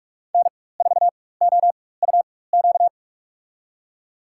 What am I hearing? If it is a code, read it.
Morse code, heard as a keyed tone, about 45 wpm, spaced as for 21 wpm, 700 Hz. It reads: N4OUQ
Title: USA Call Signs - 5 Characters - 4x Character Spacing - 45wpm